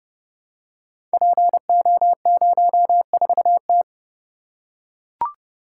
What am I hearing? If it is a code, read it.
Morse code, heard as a keyed tone, about 28 wpm, 700 Hz. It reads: PO04T
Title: Code Groups - 5 Letter:Numbers - 30wpm